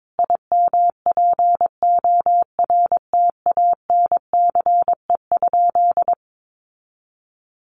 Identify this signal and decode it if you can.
Morse code, heard as a keyed tone, about 22 wpm, 700 Hz. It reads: IMPORTANCE?
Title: Sentences from Top 1200 Words 22wpm